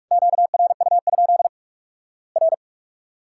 Morse code, heard as a keyed tone, about 46 wpm, 700 Hz. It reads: QRU? R
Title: Worked All Europe DX Contest - QTC Exchange - 45wpm